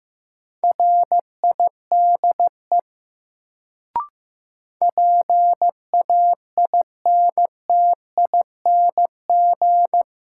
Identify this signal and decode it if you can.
Morse code, heard as a keyed tone, about 15 wpm, 700 Hz. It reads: RIDE PAINTING
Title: Sentences from Top 1200 Words - New Word Drill Alt - 15wpm